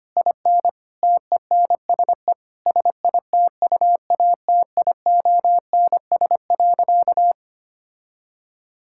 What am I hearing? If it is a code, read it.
Morse code, heard as a keyed tone, about 25 wpm, 700 Hz. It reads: IN TENSE SITUATIONS.
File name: practice / 25wpm / Difficult Sentences with Hints - 25wpm